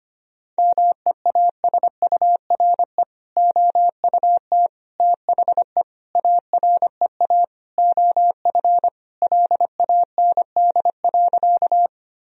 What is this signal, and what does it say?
Morse code, heard as a keyed tone, about 25 wpm, 700 Hz. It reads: MEASURE OUT THE AREA OF LAND.
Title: Sentences from Top 1000 Words - Review Part 2 - 25wpm